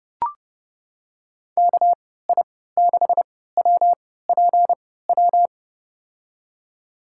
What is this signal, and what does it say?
Morse code, heard as a keyed tone, about 32 wpm, spaced as for 17 wpm, 700 Hz. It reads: KI6WPW
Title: USA Call Signs - 6 Characters - 3x Character Spacing - 30wpm